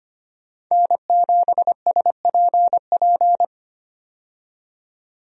Morse code, heard as a keyed tone, about 25 wpm, 700 Hz. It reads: N7SPP
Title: USA Call Signs - 5 Characters - 25wpm